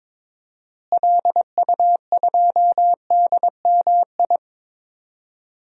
Morse code, heard as a keyed tone, about 22 wpm, 700 Hz. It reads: LU2DMI
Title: DXCC-100 Call Signs - 6 Characters - Repeated 3x - 22wpm